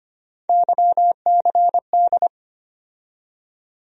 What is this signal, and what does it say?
Morse code, heard as a keyed tone, about 25 wpm, 700 Hz. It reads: YCD
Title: USA Call Sign Postfixes - 3 Letters 25wpm